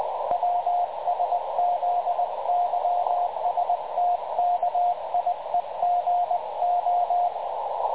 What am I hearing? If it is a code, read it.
Morse code, heard as a keyed tone, about 20 wpm, 710 Hz, with radio noise. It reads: WSZYSTKIEGO
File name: dlazaawansowanych / nagraniazpasma / websdr recording 3567.0kHz